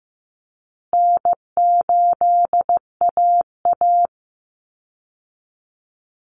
Morse code, heard as a keyed tone, about 15 wpm, 700 Hz. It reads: N8AA
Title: Call Signs Repeated 2x - 15wpm